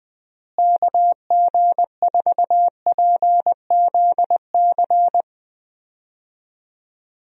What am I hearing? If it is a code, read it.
Morse code, heard as a keyed tone, about 20 wpm, 700 Hz. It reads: KG4PZC
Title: USA Call Signs - 6 Characters - Repeated 3x - 20wpm